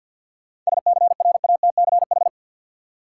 Morse code, heard as a keyed tone, about 53 wpm, 700 Hz. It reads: SCRATCH
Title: Seven Letter Words - 50wpm